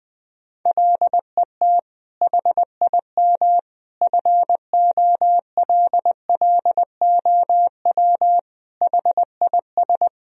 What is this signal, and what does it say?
Morse code, heard as a keyed tone, about 20 wpm, 700 Hz. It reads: LET HIM FOLLOW HIS
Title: Sentences from Top 1000 Words - Review Part 1 - 20wpm